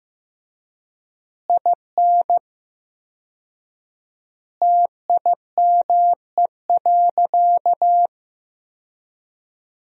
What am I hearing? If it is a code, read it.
Morse code, heard as a keyed tone, about 16 wpm, 700 Hz. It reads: IN TIME.
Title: 4x Word-Spacing - Sentences from Top 300 Words - 15wpm